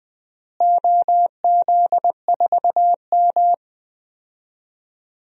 Morse code, heard as a keyed tone, about 21 wpm, 700 Hz. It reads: OZ4M
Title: DXCC-100 Call Signs - 4 Characters - Repeated 2x - 20wpm